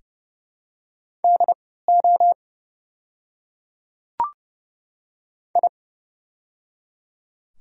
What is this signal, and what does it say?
Morse code, heard as a keyed tone, about 30 wpm, spaced as for 17 wpm, 700 Hz. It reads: DO I
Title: 3x Character Spacing - Top 100 Words - 30wpm